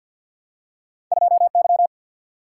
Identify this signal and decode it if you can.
Morse code, heard as a keyed tone, about 53 wpm, 700 Hz. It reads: 2Y
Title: Single-Letter Single-Number 50wpm